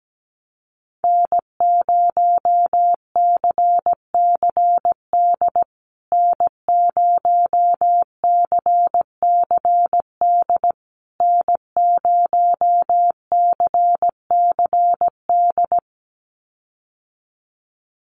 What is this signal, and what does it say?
Morse code, heard as a keyed tone, about 17 wpm, 700 Hz. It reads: N0CCD N0CCD N0CCD
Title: USA Call Signs Repeated 3x 17wpm